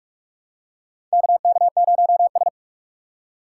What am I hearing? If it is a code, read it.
Morse code, heard as a keyed tone, about 44 wpm, 700 Hz. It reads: KK0S
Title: USA Call Signs - 4 Characters - 6x Character Spacing - 45wpm